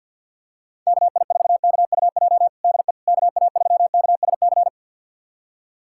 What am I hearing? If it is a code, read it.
Morse code, heard as a keyed tone, about 53 wpm, 700 Hz. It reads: KI4KUJ DE KA3KSC
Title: USA Callsign DE Callsign 50wpm